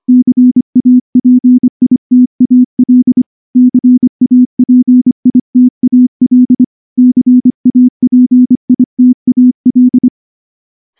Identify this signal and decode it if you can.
Morse code, heard as a keyed tone, about 24 wpm, 250 Hz. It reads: CAPITAL CAPITAL CAPITAL
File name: morse-training-25WPM